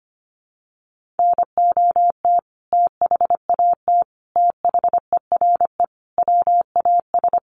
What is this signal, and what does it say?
Morse code, heard as a keyed tone, about 25 wpm, 700 Hz. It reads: NOT THAT THERE WAS
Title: Sentences from Top 1400 Words 25wpm